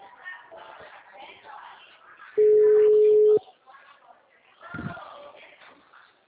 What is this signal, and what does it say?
Ringback tone (Europe, 425 Hz), with speech.